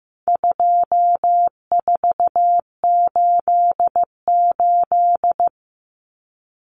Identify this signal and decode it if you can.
Morse code, heard as a keyed tone, about 15 wpm, 700 Hz. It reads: 2488
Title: Common Cut Numbers - 4-Digit 15wpm